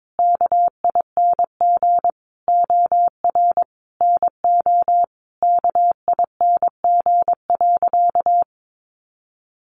Morse code, heard as a keyed tone, about 22 wpm, 700 Hz. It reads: KING OR NO KING.
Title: Sentences from Top 600 Words 22wpm